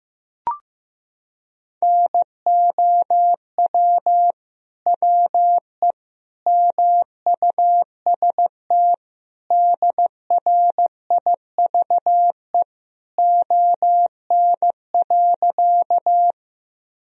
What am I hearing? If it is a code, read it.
Morse code, heard as a keyed tone, about 16 wpm, 700 Hz. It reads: NOW WE MUST DRIVE ON.
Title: Sentences from Top 1000 Words - Review Part 2 - 15wpm